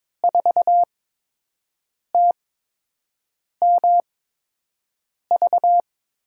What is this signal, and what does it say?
Morse code, heard as a keyed tone, about 22 wpm, spaced as for 6 wpm, 700 Hz. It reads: 4TMV
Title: USA Call Signs - 5 Characters - 8x Character Spacing - 22wpm